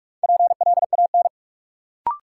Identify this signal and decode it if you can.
Morse code, heard as a keyed tone, about 42 wpm, 700 Hz. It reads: PLAN
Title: Sentences from Top 500 Words - New Word Drill - 45wpm